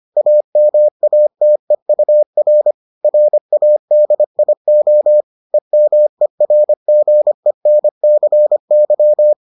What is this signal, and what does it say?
Morse code, heard as a keyed tone, about 25 wpm, 600 Hz. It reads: AMATEUR RADIO EMERGENCY